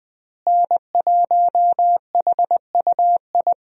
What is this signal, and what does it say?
Morse code, heard as a keyed tone, about 20 wpm, 700 Hz. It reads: N1HUI
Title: USA Call Signs - 5 Characters - Repeated 3x - 20wpm